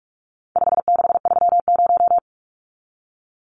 Morse code, heard as a keyed tone, about 44 wpm, 700 Hz. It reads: 5630
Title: Numbers - 4-Digit 45wpm